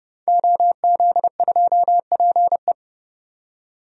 Morse code, heard as a keyed tone, about 30 wpm, 700 Hz. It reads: OZ2PE
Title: DXCC-100 Call Signs - 5 Characters - 30wpm